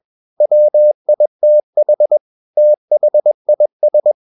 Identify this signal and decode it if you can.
Morse code, heard as a keyed tone, about 21 wpm, 600 Hz. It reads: WITH THIS
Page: CW med Gnister